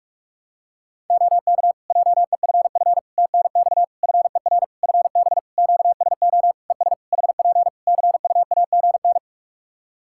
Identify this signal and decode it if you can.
Morse code, heard as a keyed tone, about 46 wpm, 700 Hz. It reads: OK JEFF TNX FER FB QSO ES HP CUAGN
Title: Practice QSO - Exchange 4 - Ending - 45wpm